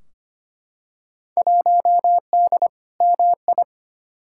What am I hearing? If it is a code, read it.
Morse code, heard as a keyed tone, about 25 wpm, 700 Hz. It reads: 1D MI
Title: Field Day Exchanges - 25wpm